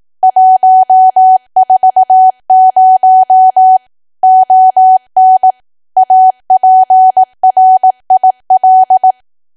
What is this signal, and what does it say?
Morse code, heard as a keyed tone, about 18 wpm, 750 Hz. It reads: 140 ON APRIL
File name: cwa / W1AW-Level2 / 18WPM